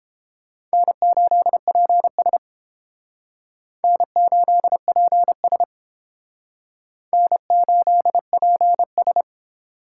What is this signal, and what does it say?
Morse code, heard as a keyed tone, about 28 wpm, 700 Hz. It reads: N8PS N8PS N8PS